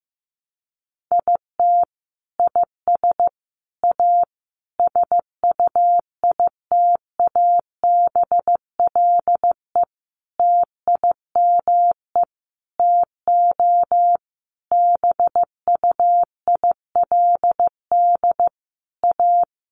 Morse code, heard as a keyed tone, about 15 wpm, 700 Hz. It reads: IT IS A SUITABLE TIME TO BUILD A
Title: Sentences from Top 1400 Words 15wpm